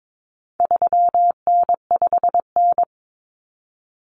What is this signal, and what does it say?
Morse code, heard as a keyed tone, about 22 wpm, 700 Hz. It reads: 3N5N
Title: Common Cut Numbers - 4-Digit 22wpm